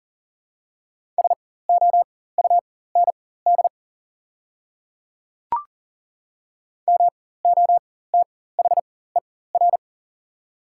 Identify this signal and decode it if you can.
Morse code, heard as a keyed tone, about 40 wpm, spaced as for 19 wpm, 700 Hz. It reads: SOUND MOTHER
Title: 4x Character Spacing - Top 200 Words - 40wpm